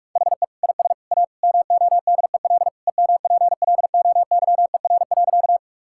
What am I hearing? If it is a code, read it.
Morse code, heard as a keyed tone, about 46 wpm, 700 Hz. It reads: HE IS A MODEL EMPLOYER.
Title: Sentences from Top 1600 Words 45wpm